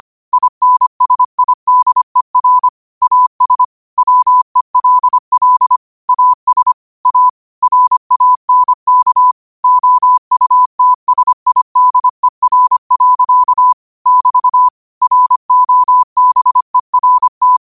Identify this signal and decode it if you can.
Morse code, heard as a keyed tone, about 25 wpm, 1000 Hz. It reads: INSIDER AS WELL AS A RANK OUTSIDER. <BT> ROBERT
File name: qotdfiles / 25wpm